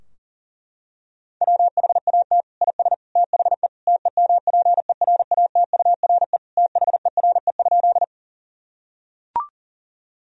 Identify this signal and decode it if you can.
Morse code, heard as a keyed tone, about 40 wpm, 700 Hz. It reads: WHAT IS THE TEMPERATURE THERE?
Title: Sentences from Top 1000 Words - Review Part 4 - 40wpm